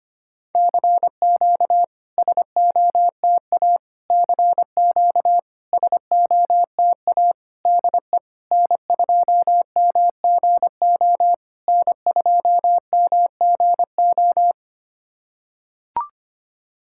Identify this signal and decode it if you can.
Morse code, heard as a keyed tone, about 25 wpm, 700 Hz. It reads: CQ SOTA CQ SOTA DE N2MGO N2MGO
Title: SOTA QSO Practice 25wpm